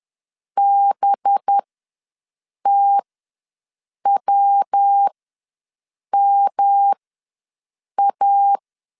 Morse code, heard as a keyed tone, about 10 wpm, spaced as for 6 wpm, 800 Hz. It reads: BTWMA
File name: LEZIONI / 5 LBVC / 5 LB ws 10wpm 3616